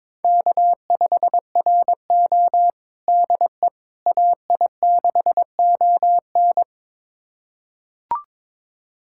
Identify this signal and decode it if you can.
Morse code, heard as a keyed tone, about 22 wpm, 700 Hz. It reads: K5RO DE AI6ON
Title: USA Callsign DE Callsign 22wpm